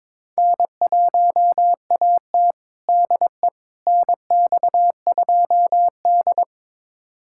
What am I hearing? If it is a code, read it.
Morse code, heard as a keyed tone, about 21 wpm, 700 Hz. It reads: N1AT DE NX2D
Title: USA Callsign DE Callsign - 4 Characters - 22wpm